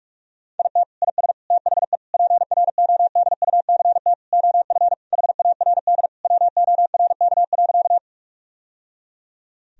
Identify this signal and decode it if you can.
Morse code, heard as a keyed tone, about 41 wpm, 700 Hz. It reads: IT IS THE PRODUCT OF HARD WORK.
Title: Sentences from Top 1000 Words - Review Part 2 - 45wpm